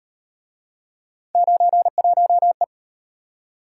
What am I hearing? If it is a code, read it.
Morse code, heard as a keyed tone, about 37 wpm, 700 Hz. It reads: 91E